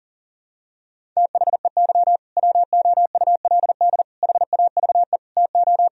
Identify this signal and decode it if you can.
Morse code, heard as a keyed tone, about 40 wpm, 700 Hz. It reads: THEY WOULD HAVE TO
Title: Sentences from Top 1000 Words - Review Part 1 - 40wpm